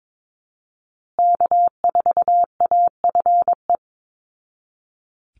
Morse code, heard as a keyed tone, about 22 wpm, 700 Hz. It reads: K4AFE
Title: Call Signs Repeated 2x - 22wpm